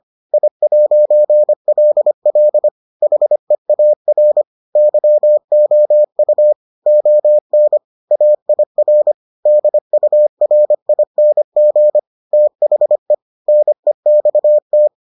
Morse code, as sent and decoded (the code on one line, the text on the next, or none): .. .----. .-.. .-.. / .... .- .-. / -.-- --- ..- / --- -. / .- .. .-. / -.. ..- .-. .. -. --. / - .... / -. -..- -
I'LL HEAR YOU ON AIR DURING THE NEXT